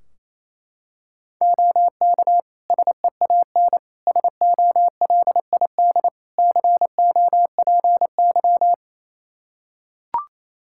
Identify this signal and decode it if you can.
Morse code, heard as a keyed tone, about 28 wpm, 700 Hz. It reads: OK SEAN SOLID COPY
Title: Practice QSO - Exchange 3 - More Info - 28wpm